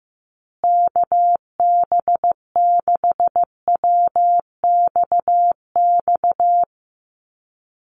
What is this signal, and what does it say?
Morse code, heard as a keyed tone, about 15 wpm, 700 Hz. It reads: KB6WXX
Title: USA Call Signs Repeated 2x 15wpm